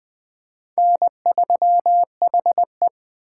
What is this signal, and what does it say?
Morse code, heard as a keyed tone, about 20 wpm, 700 Hz. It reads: N3HE
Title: Speed Racing - USA 1x2 Call Signs 17wpm